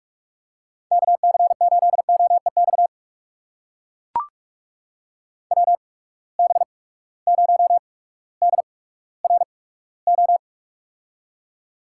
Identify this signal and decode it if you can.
Morse code, heard as a keyed tone, about 44 wpm, spaced as for 19 wpm, 700 Hz. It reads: KC8OEX WB0DRO
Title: USA Call Signs - 6 Characters - 8x Character Spacing - 45wpm